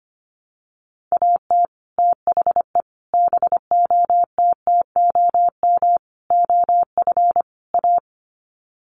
Morse code, heard as a keyed tone, about 25 wpm, 700 Hz. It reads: AT THE BOTTOM OF A
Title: Sets of 5 Words Encore2 25wpm